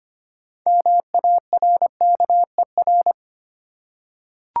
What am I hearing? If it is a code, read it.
Morse code, heard as a keyed tone, about 26 wpm, 690 Hz. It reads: MARKER
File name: practice / 25wpm / Easy Words with Hints - 25wpm